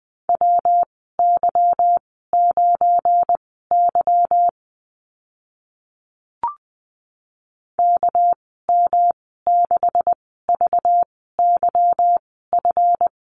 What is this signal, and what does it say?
Morse code, heard as a keyed tone, about 20 wpm, spaced as for 15 wpm, 700 Hz. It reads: WY9Y KM6VYF
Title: 207 Lesson 33g - Letter j - Call Signs 20wpm